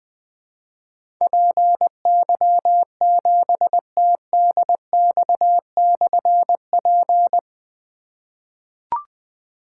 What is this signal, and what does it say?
Morse code, heard as a keyed tone, about 20 wpm, 700 Hz. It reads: PY7TDX/P
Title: Difficult Call Signs 20wpm